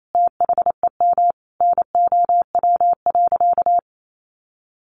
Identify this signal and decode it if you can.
Morse code, heard as a keyed tone, about 28 wpm, 700 Hz. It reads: THEM NOW.
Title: Sentences from Top 1300 Words 28wpm